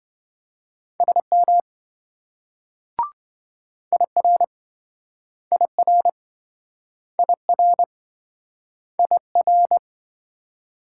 Morse code, heard as a keyed tone, about 27 wpm, 700 Hz. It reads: SM IR IR IR IR